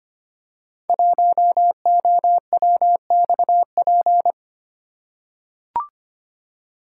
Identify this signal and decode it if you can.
Morse code, heard as a keyed tone, about 25 wpm, 700 Hz. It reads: 1OWXP
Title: Code Groups - 5 Letter:Numbers - 25wpm